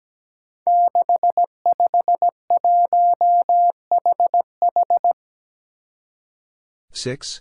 Morse code, heard as a keyed tone, about 17 wpm, 700 Hz. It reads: 651HH
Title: Five Challenging Letter-Number Combinations 17wpm